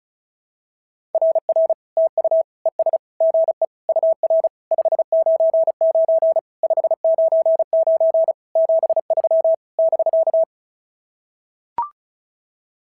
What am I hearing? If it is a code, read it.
Morse code, heard as a keyed tone, about 36 wpm, 650 Hz. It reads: RR TU ES GE UR 599 599 73 <BK>
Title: SOTA QSO Practice 35wpm